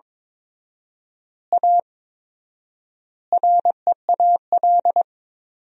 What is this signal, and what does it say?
Morse code, heard as a keyed tone, about 22 wpm, 700 Hz. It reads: A REAL
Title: 4x Word-Spacing - Sentences from Top 500 Words - 22wpm